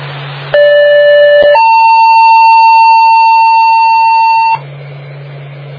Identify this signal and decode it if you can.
Two-tone page, heard as a one-second tone, then a three-second tone, about 580 then 900 Hz. It is heard from 0.5 to 4.6 s.